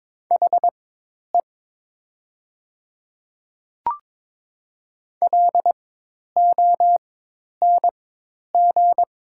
Morse code, heard as a keyed tone, about 22 wpm, spaced as for 10 wpm, 700 Hz. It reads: HE LONG
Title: 4x Character Spacing - Top 300 Words - 22wpm